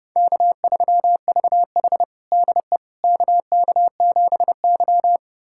Morse code, heard as a keyed tone, about 30 wpm, 700 Hz. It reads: K3VH DE KK7Y
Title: USA Callsign DE Callsign - 4 Characters - 30wpm